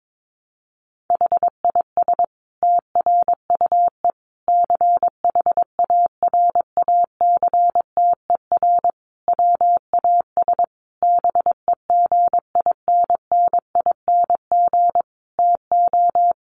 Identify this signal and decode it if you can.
Morse code, heard as a keyed tone, about 22 wpm, 700 Hz. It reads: HIS TRUE CHARACTER WAS BEGINNING TO
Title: Sentences from Top 1200 Words 22wpm